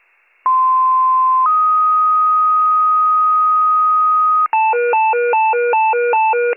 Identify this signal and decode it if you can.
Two-tone page, heard as a one-second tone, then a three-second tone, about 1040 then 1250 Hz. It is heard from 0.5 to 4.5 s.